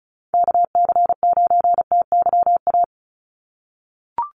Morse code, heard as a keyed tone, about 34 wpm, 700 Hz. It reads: KC9TYA
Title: NA SOTA Chaser Callsigns 35wpm